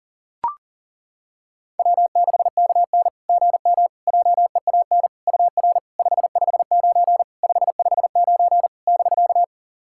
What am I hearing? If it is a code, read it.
Morse code, heard as a keyed tone, about 40 wpm, 700 Hz. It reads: W6KN GM JEAN UR 559 559 <BK>
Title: SOTA QSO Practice 40wpm